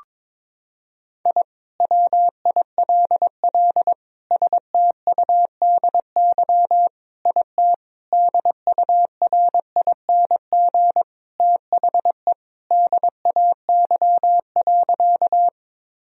Morse code, heard as a keyed tone, about 22 wpm, 700 Hz. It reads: I WILL STUDY IT DURING THE DAY.
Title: Sentences from Top 1000 Words - Review Part 2 - 22wpm